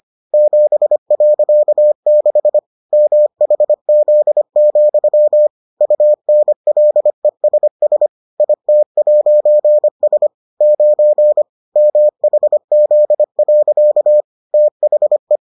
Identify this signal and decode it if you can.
Morse code, heard as a keyed tone, about 25 wpm, 600 Hz. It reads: 7.6 MHZ, UNLESS IT'S 9 MHZ. THE